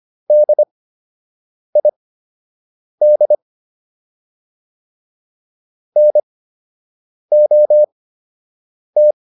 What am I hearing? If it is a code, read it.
Morse code, heard as a keyed tone, about 25 wpm, spaced as for 7 wpm, 600 Hz. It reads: DID NOT